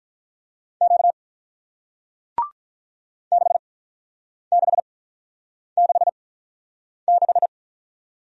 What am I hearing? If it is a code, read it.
Morse code, heard as a keyed tone, about 47 wpm, spaced as for 9 wpm, 700 Hz. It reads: Q 6666